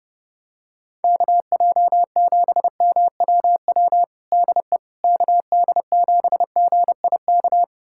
Morse code, heard as a keyed tone, about 30 wpm, 700 Hz. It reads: KJ7MWW DE KD7GIK
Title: USA Callsign DE Callsign - 6 Characters - 30wpm